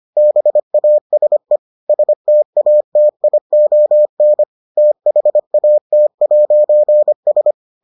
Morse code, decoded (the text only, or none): BASE STATION THAT'S